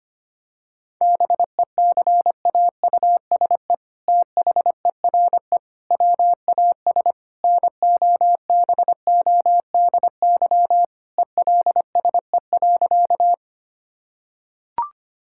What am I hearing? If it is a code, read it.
Morse code, heard as a keyed tone, about 25 wpm, 700 Hz. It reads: BECAUSE THERE WAS NOBODY ELSE.
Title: Sentences from Top 1600 Words 25wpm